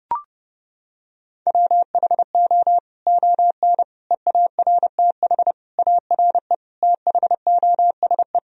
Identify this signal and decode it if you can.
Morse code, heard as a keyed tone, about 32 wpm, 700 Hz. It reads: WHO ON EARTH ARE THOSE
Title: Sentences from Top 1000 Words - Review Part 1 - 30wpm